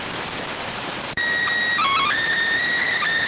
SSTV (slow-scan television), PD 290.